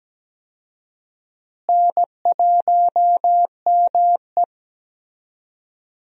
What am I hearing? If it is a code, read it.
Morse code, heard as a keyed tone, about 17 wpm, 700 Hz. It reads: N1ME
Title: USA Call Signs - 4 Characters - Repeated 3x - 17wpm